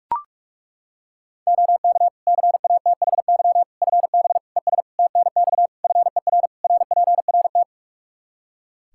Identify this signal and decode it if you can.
Morse code, heard as a keyed tone, about 41 wpm, 700 Hz. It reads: OK CATHY FB ES TNX FER RPRT
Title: K1USN SST QSO Practice 45wpm